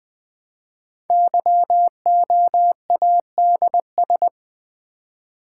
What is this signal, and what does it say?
Morse code, heard as a keyed tone, about 20 wpm, 700 Hz. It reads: YOADS
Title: Code Groups - 5 Letter:Numbers - 20wpm